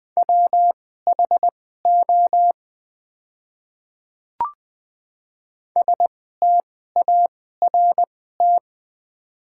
Morse code, heard as a keyed tone, about 20 wpm, spaced as for 15 wpm, 690 Hz. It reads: WHO START
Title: 2x Character Spacing - Top 500 Words - 20wpm